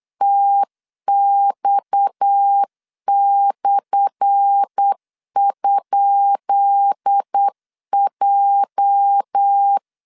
Morse code, heard as a keyed tone, about 8 wpm, 800 Hz. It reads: TX/?J
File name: LEZIONI / 10 SIGNS / 10B INT 8wpm 3534